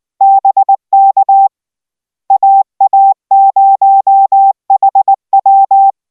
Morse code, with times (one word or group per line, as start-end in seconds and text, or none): 0.2-1.5 BK
2.3-5.9 AA0HW